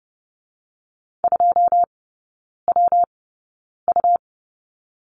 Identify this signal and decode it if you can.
Morse code, heard as a keyed tone, about 30 wpm, 700 Hz. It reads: 2 W U
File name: Lesson 19e - Letter w - Three Characters 30wpm